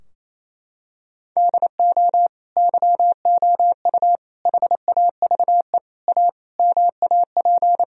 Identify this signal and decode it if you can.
Morse code, heard as a keyed tone, about 28 wpm, 700 Hz. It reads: DO YOU HAVE A MAP
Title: Sentences from Top 1000 Words - Review Part 2 - 28wpm